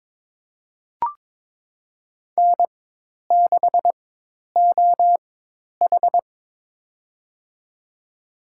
Morse code, heard as a keyed tone, about 22 wpm, spaced as for 10 wpm, 700 Hz. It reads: N6OH
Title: USA Call Signs - 4 Characters - 4x Character Spacing - 22wpm